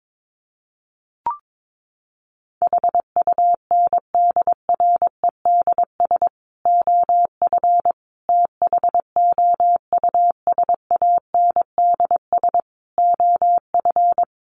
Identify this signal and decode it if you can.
Morse code, heard as a keyed tone, about 22 wpm, 700 Hz. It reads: HUNDREDS OF THOUSANDS OF
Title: Sets of 5 Words Encore 22wpm